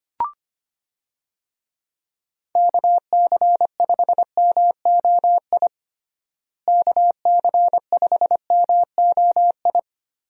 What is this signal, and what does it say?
Morse code, heard as a keyed tone, about 25 wpm, 700 Hz. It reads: KC5MOI KC5MOI
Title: USA Call Signs - 6 Characters - Repeated 3x - 25wpm